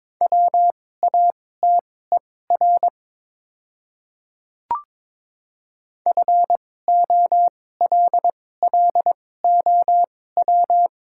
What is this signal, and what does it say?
Morse code, heard as a keyed tone, about 22 wpm, spaced as for 16 wpm, 700 Hz. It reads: WATER FOLLOW
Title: 2x Character Spacing - Top 200 Words - 22wpm